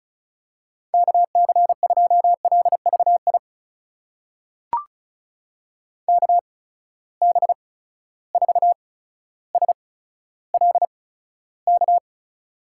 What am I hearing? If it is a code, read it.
Morse code, heard as a keyed tone, about 36 wpm, spaced as for 15 wpm, 700 Hz. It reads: KC2LVI KB4SLK
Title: USA Call Signs - 6 Characters - 8x Character Spacing - 35wpm